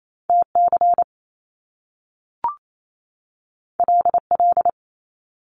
Morse code, heard as a keyed tone, about 28 wpm, 700 Hz. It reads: T C L L
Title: Top Two Letter Combinations in Words 28wpm